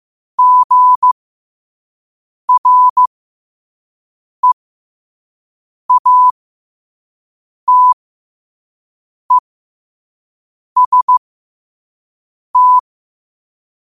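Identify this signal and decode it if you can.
Morse code, heard as a keyed tone, about 15 wpm, spaced as for 5 wpm, 1000 Hz. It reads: GREATEST
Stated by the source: Quote-of-the-day Morse code practice at 5 wpm.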